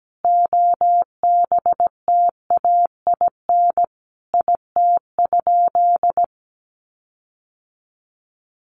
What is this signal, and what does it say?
Morse code, heard as a keyed tone, about 17 wpm, 700 Hz. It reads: OBTAIN IT?
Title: Sentences from Top 1300 Words 17wpm